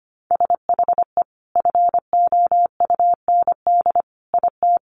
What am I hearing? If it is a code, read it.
Morse code, heard as a keyed tone, about 25 wpm, 700 Hz. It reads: SHE FOUND IT
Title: Sentences from Top 1100 Words 25wpm